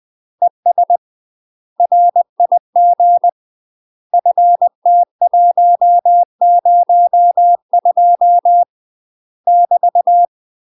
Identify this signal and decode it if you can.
Morse code, heard as a keyed tone, about 20 wpm, 700 Hz. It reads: ES RIG FT102 <BT>
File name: UK MASTER QSOs TRAINING / 20WPM / UK MASTER QSOs 0029